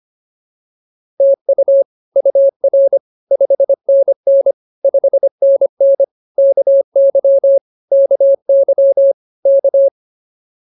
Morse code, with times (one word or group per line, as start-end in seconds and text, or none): 1.2-1.8 TU
2.2-3.0 UR
3.3-4.5 5NN
4.8-6.0 5NN
6.4-7.6 KY
7.9-9.1 KY
9.5-9.9 K